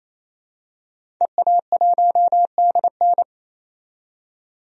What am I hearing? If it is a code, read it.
Morse code, heard as a keyed tone, about 28 wpm, 700 Hz. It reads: EA1DN
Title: DXCC-100 Call Signs - 5 Characters - 28wpm